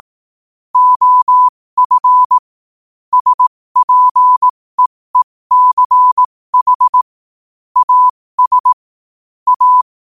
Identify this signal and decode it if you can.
Morse code, heard as a keyed tone, about 18 wpm, spaced as for 15 wpm, 1000 Hz. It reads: OF SPEECH AS A
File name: qotdfiles / 15wpm